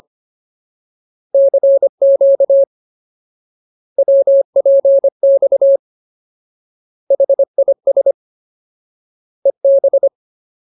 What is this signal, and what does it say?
Morse code, heard as a keyed tone, about 25 wpm, 550 Hz. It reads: CQ WPX HIS EB
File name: Ultimate Word List / 25WPM Ultimate